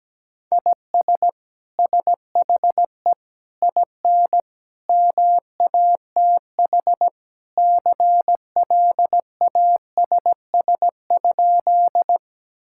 Morse code, heard as a keyed tone, about 17 wpm, 700 Hz. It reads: IS SHE IN MATH CLASS?
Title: Sentences from Top 1000 Words - Review Part 6 - 17wpm